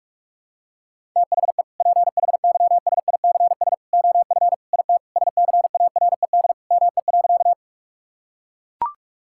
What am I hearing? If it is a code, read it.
Morse code, heard as a keyed tone, about 42 wpm, 690 Hz. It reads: THE PHYSICS OF IT SCARED ME.
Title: Sentences from Top 1000 Words - Review Part 5 - 45wpm